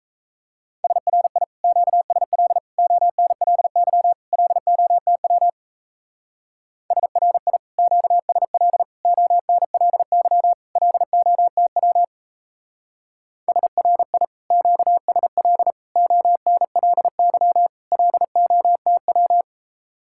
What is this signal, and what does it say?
Morse code, heard as a keyed tone, about 34 wpm, 700 Hz. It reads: SRI QSL ONLY LOTW SRI QSL ONLY LOTW SRI QSL ONLY LOTW
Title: Speed-Racing Common QSO Phrases Encore - 28wpm